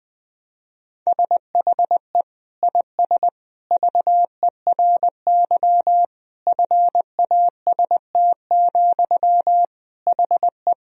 Morse code, heard as a keyed tone, about 20 wpm, 700 Hz. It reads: SHE IS VERY FAST, HE
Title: Sentences from Top 1000 Words - Review Part 2 - 20wpm